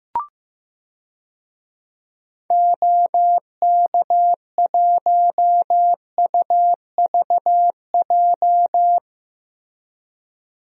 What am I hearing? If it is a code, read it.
Morse code, heard as a keyed tone, about 16 wpm, 700 Hz. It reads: OK1UVJ
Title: DXCC-100 Call Signs - 6 Characters - Repeated 2x - 15wpm